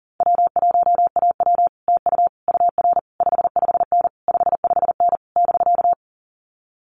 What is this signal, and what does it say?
Morse code, heard as a keyed tone, about 40 wpm, 700 Hz. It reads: W1AW TU UR 55N 55N <BK>
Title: Common QSO Phrases Encore 40wpm